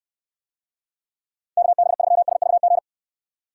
Morse code, heard as a keyed tone, about 69 wpm, 700 Hz. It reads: KB3SVG